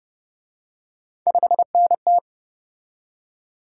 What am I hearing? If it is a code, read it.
Morse code, heard as a keyed tone, about 33 wpm, 700 Hz. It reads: 5NT